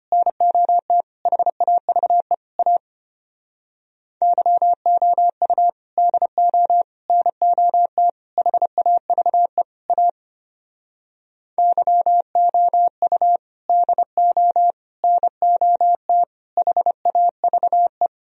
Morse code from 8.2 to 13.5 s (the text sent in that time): HAVE A YOU